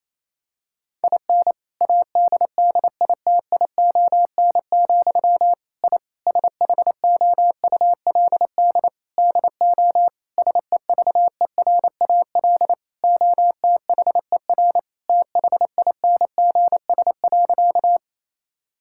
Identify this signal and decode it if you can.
Morse code, heard as a keyed tone, about 28 wpm, 700 Hz. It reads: IN ADDITION, I SHOULD DO SEVERAL OTHER THINGS.
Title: Sentences from Top 1000 Words - Review Part 6 - 28wpm